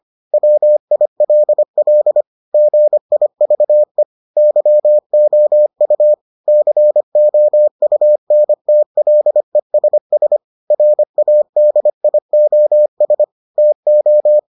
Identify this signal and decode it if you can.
Morse code, heard as a keyed tone, about 25 wpm, 600 Hz. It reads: WILL GIVE YOU COUNTLESS RADIOS TO